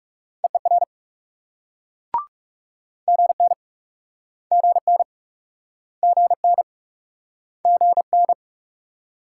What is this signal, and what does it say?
Morse code, heard as a keyed tone, about 37 wpm, 700 Hz. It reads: EER GN GN GN GN
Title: Speed-Racing Multi-letter Phonemes - 30wpm